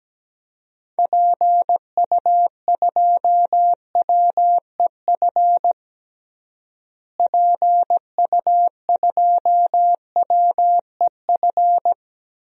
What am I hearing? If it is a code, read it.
Morse code, heard as a keyed tone, about 17 wpm, 700 Hz. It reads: PU2WEF PU2WEF
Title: DXCC-100 Call Signs - 6 Characters - Repeated 3x - 17wpm